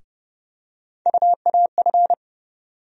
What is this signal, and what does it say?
Morse code, heard as a keyed tone, about 30 wpm, 700 Hz. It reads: UAF
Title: USA Call Sign Postfixes - 3 Letters 30wpm